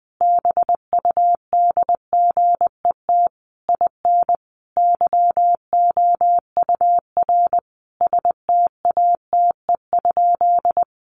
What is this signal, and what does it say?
Morse code, heard as a keyed tone, about 20 wpm, 700 Hz. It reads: BUDGET IN YOUR STATE?
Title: Sentences from Top 1200 Words 20wpm